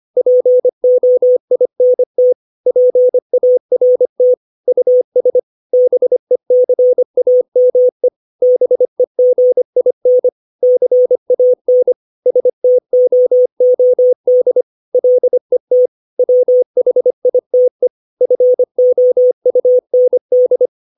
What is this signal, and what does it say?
Morse code, heard as a keyed tone, about 25 wpm, 500 Hz. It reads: POINT PART US BECAME BEGIN CAN STOOD LET WHITE FOUND
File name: 632 words / 632 Words 25WPM